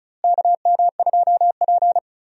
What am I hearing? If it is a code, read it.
Morse code, heard as a keyed tone, about 36 wpm, 700 Hz. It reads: KM2P
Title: USA Call Signs - 4 Characters - Repeated 3x - 35wpm